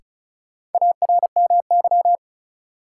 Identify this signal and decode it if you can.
Morse code, heard as a keyed tone, about 36 wpm, 700 Hz. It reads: ARMY
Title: Sentences from Top 800 Words - New Word Drill - 35wpm